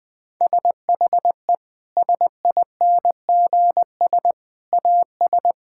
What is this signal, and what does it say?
Morse code, heard as a keyed tone, about 20 wpm, 700 Hz. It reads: SHE SINGS AS
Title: Sentences with 3-Word Phrases - 20wpm